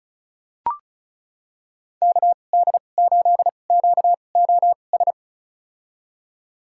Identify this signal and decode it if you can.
Morse code, heard as a keyed tone, about 36 wpm, spaced as for 25 wpm, 700 Hz. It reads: KD8QOS
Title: USA Call Signs - 6 Characters - 2x Character Spacing - 35wpm